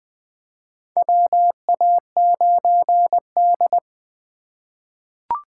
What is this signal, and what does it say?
Morse code, heard as a keyed tone, about 20 wpm, 700 Hz. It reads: WA9D
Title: USA Call Signs - 4 Characters - 3x Character Spacing - 20wpm